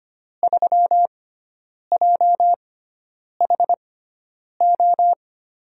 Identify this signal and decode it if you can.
Morse code, heard as a keyed tone, about 25 wpm, spaced as for 9 wpm, 700 Hz. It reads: 3JHO
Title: USA Call Signs - 6 Characters - 6x Character Spacing - 25wpm